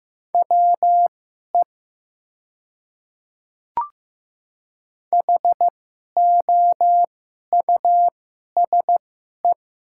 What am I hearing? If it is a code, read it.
Morse code, heard as a keyed tone, about 15 wpm, spaced as for 11 wpm, 700 Hz. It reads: WE HOUSE
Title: 2x Character Spacing - Top 300 Words - 15wpm